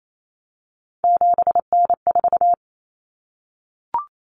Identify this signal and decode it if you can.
Morse code, heard as a keyed tone, about 28 wpm, 700 Hz. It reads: 7N4
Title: Common Cut Numbers - 3-Digit 28wpm